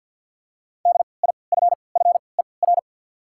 Morse code, heard as a keyed tone, about 53 wpm, 700 Hz. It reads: D I F F E R
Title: 3x Character Spacing - Top 200 Words - 50wpm